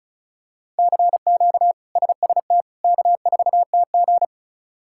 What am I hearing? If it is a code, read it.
Morse code, heard as a keyed tone, about 36 wpm, 700 Hz. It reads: CQ SST K4TG
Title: K1USN SST QSO Practice - 35wpm